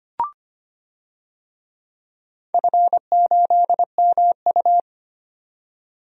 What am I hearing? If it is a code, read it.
Morse code, heard as a keyed tone, about 25 wpm, 700 Hz. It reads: F8MU
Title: DXCC-100 Call Signs - 4 Characters - 25wpm